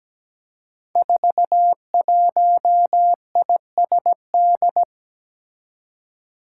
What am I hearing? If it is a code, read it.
Morse code, heard as a keyed tone, about 17 wpm, 700 Hz. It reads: 41ISD
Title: Five Challenging Letter-Number Combinations 17wpm